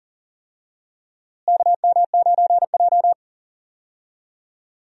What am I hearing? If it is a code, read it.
Morse code, heard as a keyed tone, about 38 wpm, 700 Hz. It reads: KM9J
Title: USA Call Signs - 4 Characters - Repeated 3x - 40wpm